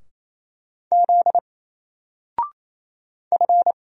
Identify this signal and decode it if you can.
Morse code, heard as a keyed tone, about 28 wpm, 700 Hz. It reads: Z F
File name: Single-Letter-Number-28wpm